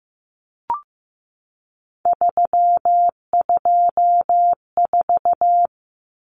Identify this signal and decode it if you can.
Morse code, heard as a keyed tone, about 15 wpm, 700 Hz. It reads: 324
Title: Common Cut Numbers - 3-Digit 15wpm